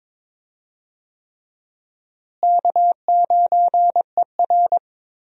Morse code, heard as a keyed tone, about 22 wpm, 700 Hz. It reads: K9ER
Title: USA Call Signs - 4 Characters - Repeated 3x - 22wpm